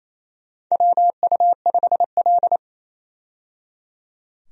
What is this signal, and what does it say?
Morse code, heard as a keyed tone, about 27 wpm, 700 Hz. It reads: WU5L
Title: USA Call Signs - 4 Characters - 28wpm